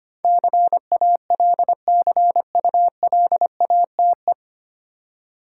Morse code, heard as a keyed tone, about 25 wpm, 700 Hz. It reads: CALCULATE
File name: Nine Letter Words - 25wpm